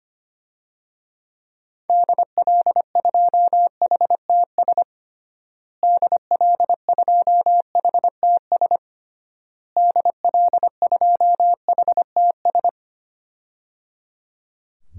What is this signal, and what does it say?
Morse code, heard as a keyed tone, about 25 wpm, 700 Hz. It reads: DL2HTS DL2HTS DL2HTS
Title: DXCC-100 Call Signs - 6 Characters - Repeated 3x - 25wpm